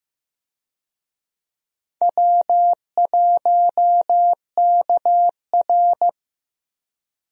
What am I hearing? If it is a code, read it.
Morse code, heard as a keyed tone, about 15 wpm, 700 Hz. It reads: W1KR
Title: USA Call Signs - 4 Characters - Repeated 3x - 15wpm